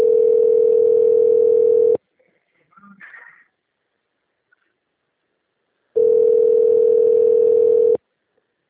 Ringback tone (North America), 440 + 480 Hz.
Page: Boa Viagem - RE